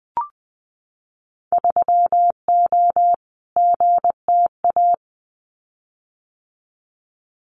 Morse code, heard as a keyed tone, about 20 wpm, 700 Hz. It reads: 3O GTA
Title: Winter Field Day Exchanges Encore 20wpm